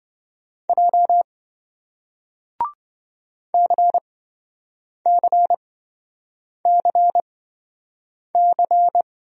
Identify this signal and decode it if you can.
Morse code, heard as a keyed tone, about 26 wpm, spaced as for 7 wpm, 700 Hz. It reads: J CCCC